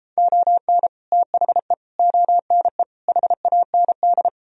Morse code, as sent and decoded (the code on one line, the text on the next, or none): --- -. / - .... / --- -. / .... .- -. -..
ON THE ONE HAND